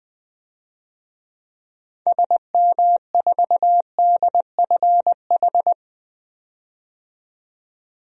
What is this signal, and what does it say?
Morse code, heard as a keyed tone, about 20 wpm, 700 Hz. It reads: SM4DFH
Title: DXCC-100 Call Signs - 6 Characters - 20wpm